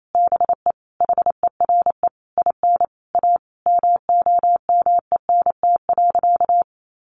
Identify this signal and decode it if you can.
Morse code, heard as a keyed tone, about 28 wpm, 700 Hz. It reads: BE HERE IN A MOMENT.
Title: Sentences from Top 1000 Words 28wpm